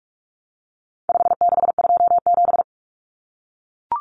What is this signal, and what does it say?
Morse code, heard as a keyed tone, about 44 wpm, 700 Hz. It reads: 5627
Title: Common Cut Numbers - 4-Digit 45wpm